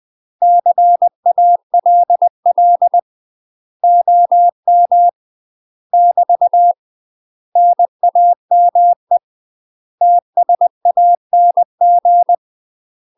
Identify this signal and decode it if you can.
Morse code, heard as a keyed tone, about 20 wpm, 700 Hz. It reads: CALL OM <BT> NAME TSANG
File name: UK MASTER QSOs TRAINING / 20WPM / UK MASTER QSOs 0028